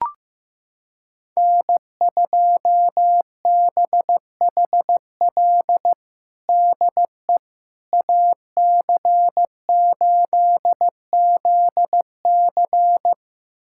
Morse code, heard as a keyed tone, about 15 wpm, 700 Hz. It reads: N2BHL DE AC8ZC
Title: USA Callsign DE Callsign - 5 Characters - 15wpm